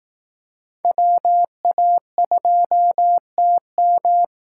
Morse code, heard as a keyed tone, about 19 wpm, 700 Hz. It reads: WA2TM